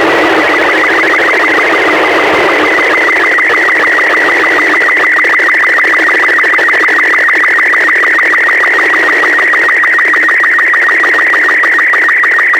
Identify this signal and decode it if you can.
SSTV (slow-scan television), Robot 36.
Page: prijem-sstv-obrazku-z-iss